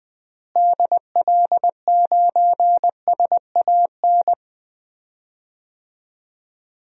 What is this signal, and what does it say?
Morse code, heard as a keyed tone, about 20 wpm, 700 Hz. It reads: DL9SAN
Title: DXCC-100 Call Signs - 6 Characters - Repeated 2x - 20wpm